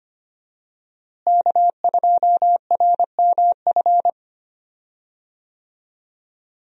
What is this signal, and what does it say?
Morse code, heard as a keyed tone, about 25 wpm, 700 Hz. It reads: K2RMF
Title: USA Call Signs - 5 Characters - Repeated 2x - 25wpm